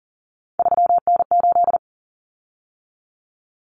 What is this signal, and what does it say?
Morse code, heard as a keyed tone, about 40 wpm, 700 Hz. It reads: 3N8
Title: Common Cut Numbers - 3-Digit 40wpm